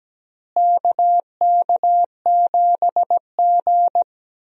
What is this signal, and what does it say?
Morse code, heard as a keyed tone, about 17 wpm, 700 Hz. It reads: KK7G
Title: USA Call Signs - 4 Characters - Repeated 3x - 17wpm